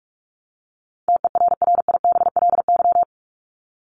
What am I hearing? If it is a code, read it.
Morse code, heard as a keyed tone, about 44 wpm, 700 Hz. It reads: TERRIBLY
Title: Top 1751-2000 Words 45wpm